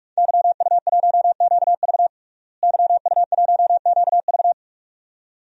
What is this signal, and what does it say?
Morse code, heard as a keyed tone, about 43 wpm, 700 Hz. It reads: YU1QV YU1QV
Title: DXCC-100 Call Signs - 5 Characters - Repeated 2x - 45wpm